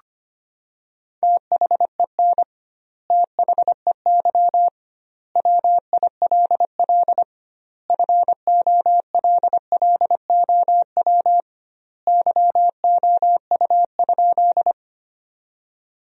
Morse code, heard as a keyed tone, about 25 wpm, 700 Hz. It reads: THEN THEY WILL FOLLOW YOU?
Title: 2x Word-Spacing - Sentences from Top 200 Words - 25wpm